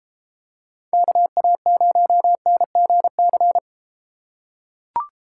Morse code, heard as a keyed tone, about 32 wpm, 700 Hz. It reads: KA0NGC